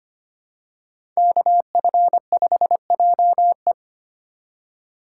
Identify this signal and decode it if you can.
Morse code, heard as a keyed tone, about 25 wpm, 700 Hz. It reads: KF5JE
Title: USA Call Signs - 2x Character Spacing - 25wpm